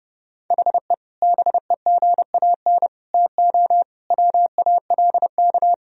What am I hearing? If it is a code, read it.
Morse code, heard as a keyed tone, about 30 wpm, 700 Hz. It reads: HE BEGAN TO WALK
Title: Sentences from Top 1000 Words - Review Part 1 - 30wpm